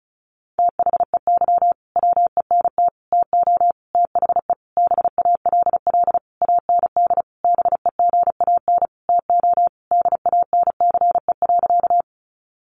Morse code, heard as a keyed tone, about 34 wpm, 700 Hz. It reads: THEY WENT TO THE BALL AND BEGAN TO DANCE.
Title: Sentences from Top 900 Words 35wpm